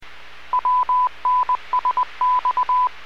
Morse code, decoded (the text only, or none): WNSX